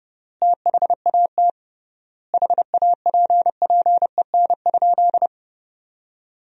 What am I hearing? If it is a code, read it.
Morse code, heard as a keyed tone, about 32 wpm, 700 Hz. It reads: THAT HAPPEN?
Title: 3x Word-Spacing - Sentences from Top 400 Words - 30wpm